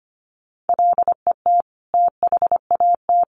Morse code, heard as a keyed tone, about 25 wpm, 700 Hz. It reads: LET THAT
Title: Sentences from Top 400 Words 25wpm